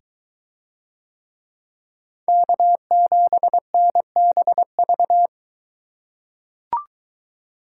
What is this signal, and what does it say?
Morse code, heard as a keyed tone, about 22 wpm, 700 Hz. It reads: K7NBV